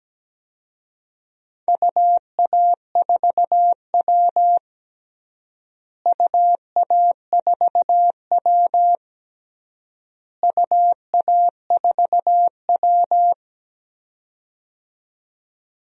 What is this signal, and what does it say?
Morse code, heard as a keyed tone, about 17 wpm, 700 Hz. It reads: UA4W UA4W UA4W
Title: DXCC-100 Call Signs - 4 Characters - Repeated 3x - 17wpm